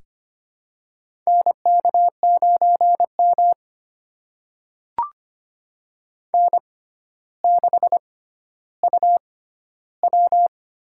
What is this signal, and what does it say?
Morse code, heard as a keyed tone, about 25 wpm, spaced as for 13 wpm, 700 Hz. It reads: NK9M N6UW
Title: USA Call Signs - 4 Characters - 6x Character Spacing - 25wpm